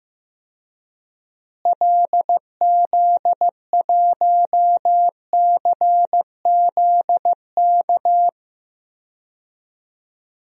Morse code, heard as a keyed tone, about 15 wpm, 700 Hz. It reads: LZ1CZK
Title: DXCC-100 Call Signs - 6 Characters - Repeated 3x - 15wpm